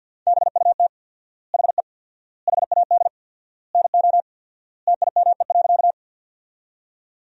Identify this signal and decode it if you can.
Morse code, heard as a keyed tone, about 50 wpm, 700 Hz. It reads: BUT HE HAD NO TIME.
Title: 4x Word-Spacing - Sentences from Top 100 Words - 50wpm